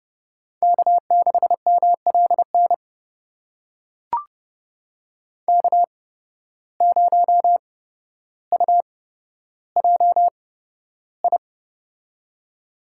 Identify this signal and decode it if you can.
Morse code, heard as a keyed tone, about 30 wpm, spaced as for 13 wpm, 700 Hz. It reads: K6MLN K0UJI
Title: USA Call Signs - 8x Character Spacing - 30wpm